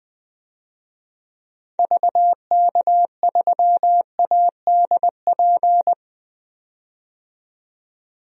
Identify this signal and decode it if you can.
Morse code, heard as a keyed tone, about 20 wpm, 700 Hz. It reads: VK3ADP
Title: DXCC-100 Call Signs - 6 Characters - 20wpm